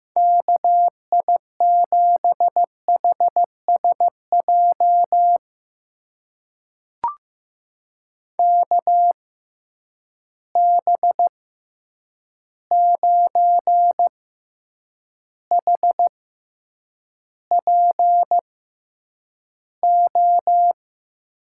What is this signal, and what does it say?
Morse code, heard as a keyed tone, about 15 wpm, spaced as for 8 wpm, 700 Hz. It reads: KI7HSJ KB9HPO
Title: USA Call Signs - 6 Characters - 6x Character Spacing - 15wpm